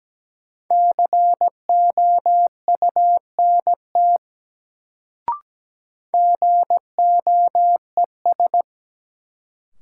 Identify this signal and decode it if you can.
Morse code, heard as a keyed tone, about 17 wpm, 700 Hz. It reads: COUNT GOES
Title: Sentences from Top 1500 Words - New Word Drill Alt - 17wpm